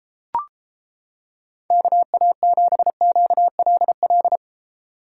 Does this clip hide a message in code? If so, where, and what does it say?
Yes: Morse code from 1.7 to 4.4 s, heard as a keyed tone, about 29 wpm, 700 Hz. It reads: KA7QLL